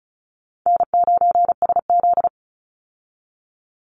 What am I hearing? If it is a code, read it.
Morse code, heard as a keyed tone, about 34 wpm, 700 Hz. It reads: N9SZ
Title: Easy Call Signs Repeated 2x - 35wpm